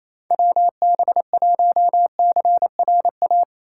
Morse code, heard as a keyed tone, about 28 wpm, 700 Hz. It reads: WB1CRA
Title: USA Call Signs - 6 Characters - 4x Character Spacing - 28wpm